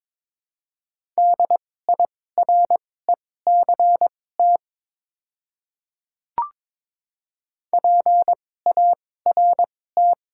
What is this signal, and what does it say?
Morse code, heard as a keyed tone, about 22 wpm, spaced as for 16 wpm, 700 Hz. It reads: DIRECT PART
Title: 2x Character Spacing - Top 500 Words - 22wpm